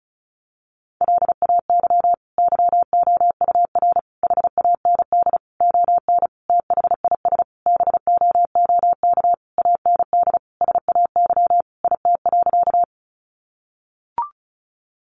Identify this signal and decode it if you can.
Morse code, heard as a keyed tone, about 34 wpm, 700 Hz. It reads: LAY YOUR HAND ON THIS BOOK AND SAY IT.
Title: Sentences from Top 400 Words 35wpm